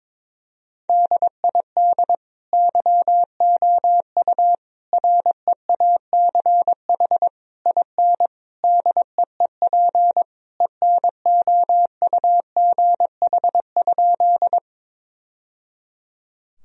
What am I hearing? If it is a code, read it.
Morse code, heard as a keyed tone, about 22 wpm, 700 Hz. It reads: DID YOU REACH IN DEEP ENOUGH?
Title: Sentences from Top 1000 Words - Review Part 3 - 22wpm